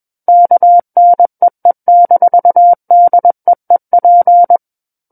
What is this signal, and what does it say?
Morse code, heard as a keyed tone, about 21 wpm, 700 Hz. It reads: KNEE-DEEP
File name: Book One / 12-Chapter 12 - 21 WPM